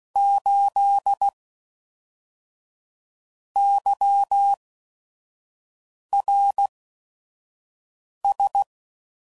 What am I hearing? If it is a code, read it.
Morse code, heard as a keyed tone, about 16 wpm, spaced as for 5 wpm, 800 Hz. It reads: 8 YRS
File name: CW Code Groups @5WPM / QSOs / kc7sac